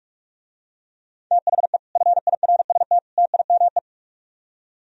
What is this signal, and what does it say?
Morse code, heard as a keyed tone, about 47 wpm, 700 Hz. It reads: THE FIRST TIME